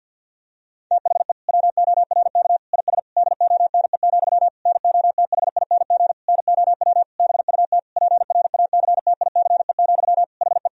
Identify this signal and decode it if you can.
Morse code, heard as a keyed tone, about 50 wpm, 700 Hz. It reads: THE WORK IS DONE, NOTHING NOW BUT PRACTICE, HE
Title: Sentences from Top 1000 Words - Review Part 4 - 50wpm